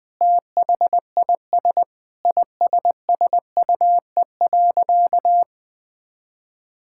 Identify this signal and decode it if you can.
Morse code, heard as a keyed tone, about 19 wpm, 700 Hz. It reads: THIS ISSUE.
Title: Sentences with 7-Word Phrases - 20wpm